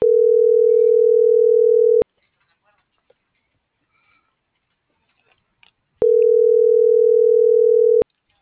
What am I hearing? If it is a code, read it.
Ringback tone (North America), 440 + 480 Hz.
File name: CENTRAL MULTIMODAL / out josimar.cruz - .59380